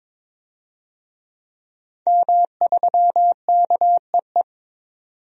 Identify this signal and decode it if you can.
Morse code, heard as a keyed tone, about 22 wpm, 700 Hz. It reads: M3KEE
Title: DXCC-100 Call Signs - 5 Characters - 22wpm